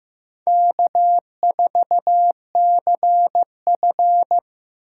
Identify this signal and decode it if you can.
Morse code, heard as a keyed tone, about 15 wpm, 700 Hz. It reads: K4CF
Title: USA Call Signs - 4 Characters - 2x Character Spacing - 15wpm